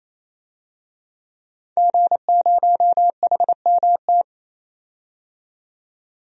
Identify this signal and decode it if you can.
Morse code, heard as a keyed tone, about 27 wpm, 700 Hz. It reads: G0HMT
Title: DXCC-100 Call Signs - 5 Characters - 28wpm